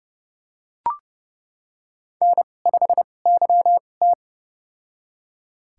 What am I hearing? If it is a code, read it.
Morse code, heard as a keyed tone, about 32 wpm, spaced as for 22 wpm, 700 Hz. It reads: N5YT
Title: USA Call Signs - 4 Characters - 2x Character Spacing - 30wpm